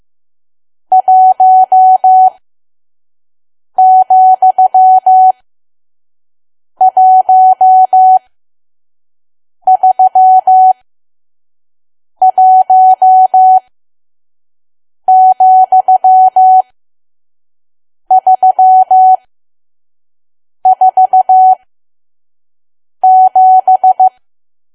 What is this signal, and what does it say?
Morse code, heard as a keyed tone, about 15 wpm, spaced as for 5 wpm, 750 Hz. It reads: 1,131,347
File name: Archive / 5 WPM files / PM